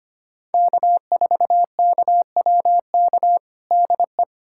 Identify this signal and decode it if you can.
Morse code, heard as a keyed tone, about 25 wpm, 700 Hz. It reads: K4KWK DE
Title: USA Callsign DE Callsign - 5 Characters - 25wpm